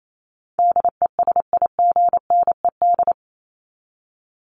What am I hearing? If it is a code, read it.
Morse code, heard as a keyed tone, about 28 wpm, 700 Hz. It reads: DESIGNED
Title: ED Suffix - Top 100 Words with Base Words 28wpm